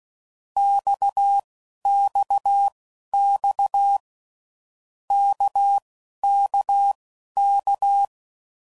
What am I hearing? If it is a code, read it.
Morse code, heard as a keyed tone, about 16 wpm, spaced as for 11 wpm, 800 Hz. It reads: XXX KKK
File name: Groups / Basic Letter Groups / RKPX